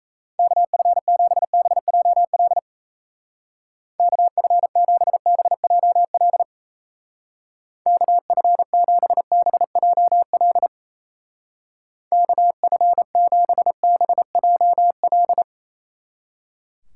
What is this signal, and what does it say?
Morse code, heard as a keyed tone, about 33 wpm, 700 Hz. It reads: KF7BJL KF7BJL KF7BJL KF7BJL
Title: Speed Racing - USA 2x3 Call Signs 28wpm